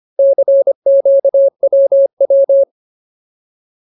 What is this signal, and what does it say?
Morse code, heard as a keyed tone, about 25 wpm, 550 Hz. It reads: CQWW
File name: Ultimate Word List / 25WPM Ultimate